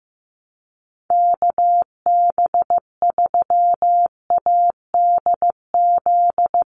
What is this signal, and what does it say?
Morse code, heard as a keyed tone, about 15 wpm, 700 Hz. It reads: KB3ADZ
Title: USA Call Signs 15wpm